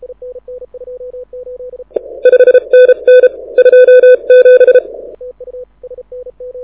Morse code, heard as a keyed tone, about 31 wpm, 2500 Hz. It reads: 5NN 27
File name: audio zawody / IARU 2015 CW SN7Q